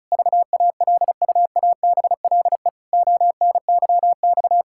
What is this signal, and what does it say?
Morse code, heard as a keyed tone, about 36 wpm, 700 Hz. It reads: VALUABLE ONYX
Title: Pangram Challenge v2 - 35wpm